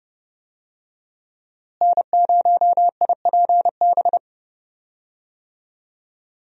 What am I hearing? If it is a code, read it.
Morse code, heard as a keyed tone, about 30 wpm, 700 Hz. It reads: N0IPB
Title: USA Call Signs - 5 Characters - Repeated 2x - 30wpm